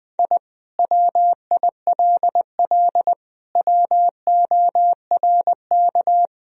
Morse code, heard as a keyed tone, about 19 wpm, 700 Hz. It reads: I WILL WORK
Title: Sentences from Top 1000 Words - Review Part 4 - 20wpm